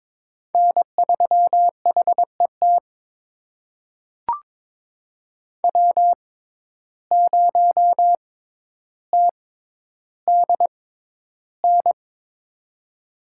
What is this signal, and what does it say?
Morse code, heard as a keyed tone, about 23 wpm, spaced as for 11 wpm, 700 Hz. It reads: N3HET W0TDN
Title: USA Call Signs - 5 Characters - 6x Character Spacing - 22wpm